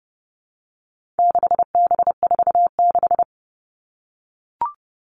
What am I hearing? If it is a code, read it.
Morse code, heard as a keyed tone, about 30 wpm, 700 Hz. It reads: 6B46
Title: All Cut Numbers - 4-Digit 30wpm